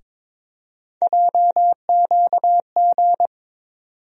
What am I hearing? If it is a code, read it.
Morse code, heard as a keyed tone, about 21 wpm, 700 Hz. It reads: JQG
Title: USA Call Sign Postfixes - 3 Letters 22wpm